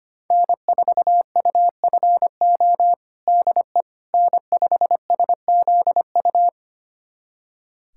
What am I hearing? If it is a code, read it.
Morse code, heard as a keyed tone, about 25 wpm, 700 Hz. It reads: N4UFO DE N5SZU
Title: USA Callsign DE Callsign - 5 Characters - 25wpm